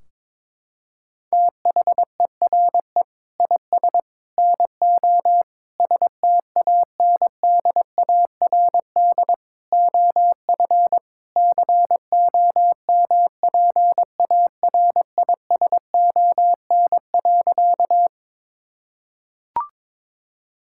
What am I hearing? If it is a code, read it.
Morse code, heard as a keyed tone, about 22 wpm, 700 Hz. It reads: THERE IS NO STANDARD OF COMPARISON.
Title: Sentences from Top 1600 Words 22wpm